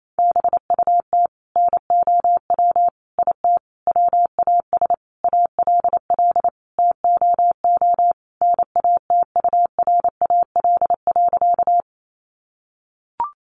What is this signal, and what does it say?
Morse code, heard as a keyed tone, about 28 wpm, 700 Hz. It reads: BUT NOW IT WAS ALL TOO NATURAL.
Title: Sentences from Top 700 Words 28wpm